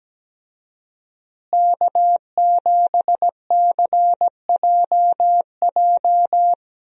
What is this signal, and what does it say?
Morse code, heard as a keyed tone, about 17 wpm, 700 Hz. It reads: K7CJJ
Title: USA Call Signs - 5 Characters - Repeated 3x - 17wpm